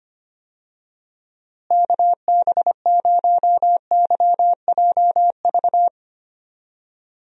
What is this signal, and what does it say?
Morse code, heard as a keyed tone, about 25 wpm, 700 Hz. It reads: KB0YJV
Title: USA Call Signs - 6 Characters - Repeated 2x - 25wpm